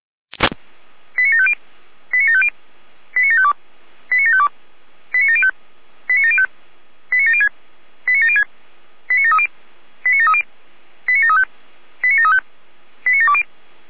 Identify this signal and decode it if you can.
Five-tone selective call (ZVEI), heard as quick runs of five tones, tones of about 70 ms. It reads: 89750 89750 89752 89752 89705 89705 89706 89706 89730 89730 89735 89735 89720